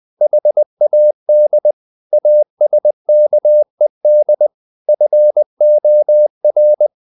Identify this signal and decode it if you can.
Morse code, heard as a keyed tone, about 20 wpm, 600 Hz. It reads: HAD ASKED FOR